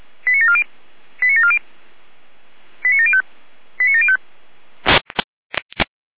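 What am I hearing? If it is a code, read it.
Five-tone selective call (ZVEI), heard as quick runs of five tones, tones of about 70 ms. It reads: 89740 89740 89705 89705
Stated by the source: ZVEI five-tone selective call received by SDR, code 89590.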